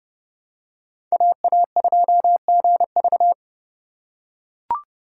Morse code, heard as a keyed tone, about 30 wpm, 700 Hz. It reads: AA2GV
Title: USA Call Signs - 5 Characters - 2x Character Spacing - 30wpm